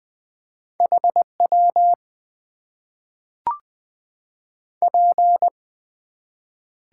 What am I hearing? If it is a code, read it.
Morse code, heard as a keyed tone, about 21 wpm, 700 Hz. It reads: HW P
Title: Top 90 QSO Elements - 8x Character Spacing - 20wpm